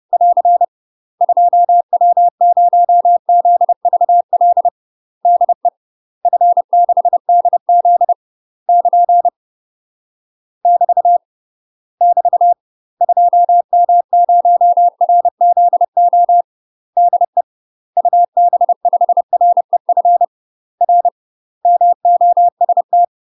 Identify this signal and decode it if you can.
Morse code, heard as a keyed tone, about 30 wpm, 700 Hz. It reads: <AR> 2W0ZVL DE F6DZ <KN> <BT> <BT> 2M0RZO DE UB5REF R MOST
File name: UK MASTER QSOs TRAINING / 30WPM / UK MASTER QSOs 0002